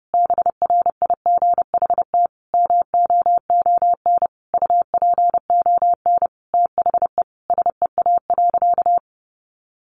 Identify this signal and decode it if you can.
Morse code, heard as a keyed tone, about 30 wpm, 700 Hz. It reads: BRIGHT MOON UPON THE SEA.
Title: Sentences from Top 1500 Words 30wpm